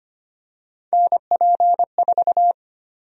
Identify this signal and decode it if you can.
Morse code, heard as a keyed tone, about 25 wpm, 700 Hz. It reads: NP4
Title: USA Call Sign Prefixes - Repeated 2x - 25wpm